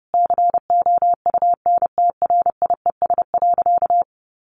Morse code, heard as a keyed tone, about 30 wpm, 700 Hz. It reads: COUNTRIES.
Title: A Sound Of Thunder 30wpm